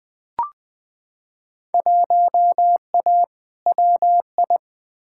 Morse code, heard as a keyed tone, about 20 wpm, 700 Hz. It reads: 1A WI
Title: Field Day Exchanges - 20wpm